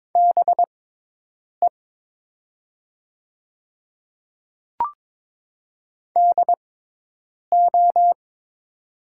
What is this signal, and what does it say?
Morse code, heard as a keyed tone, about 22 wpm, spaced as for 8 wpm, 700 Hz. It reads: BE DO
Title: 6x Character Spacing - Top 100 Words - 22wpm